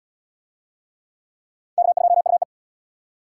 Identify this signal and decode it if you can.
Morse code, heard as a keyed tone, about 76 wpm, 700 Hz. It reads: K2UE